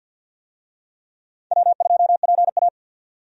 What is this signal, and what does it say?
Morse code, heard as a keyed tone, about 46 wpm, 700 Hz. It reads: W2PA
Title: USA Call Signs - 4 Characters - 50wpm